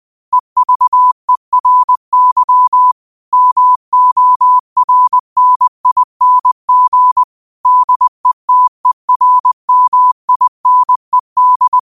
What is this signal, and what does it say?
Morse code, heard as a keyed tone, about 20 wpm, 1000 Hz. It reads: EVERY MORNING DETERMINED
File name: qotdfiles / 20wpm